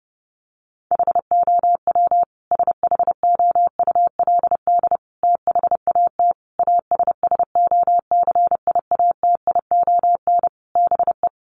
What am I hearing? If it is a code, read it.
Morse code, heard as a keyed tone, about 30 wpm, 700 Hz. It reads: HOW SHOULD THAT ASSOCIATION BE
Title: Sentences from Top 1300 Words 30wpm